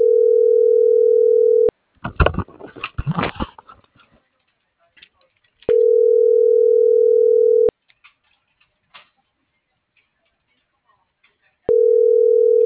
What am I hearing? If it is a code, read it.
Ringback tone (North America), 440 + 480 Hz.